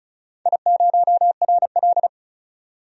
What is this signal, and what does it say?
Morse code, heard as a keyed tone, about 36 wpm, 700 Hz. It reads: I0RL
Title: DXCC-100 Call Signs - 4 Characters - 35wpm